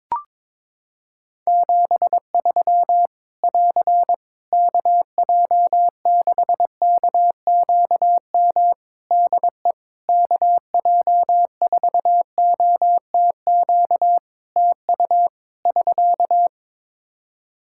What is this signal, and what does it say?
Morse code, heard as a keyed tone, about 21 wpm, 700 Hz. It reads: 73 <AR> KJ6KQM DE KJ4OTQ TU <SK>
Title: Practice QSO - Exchange 4 - Ending - 22wpm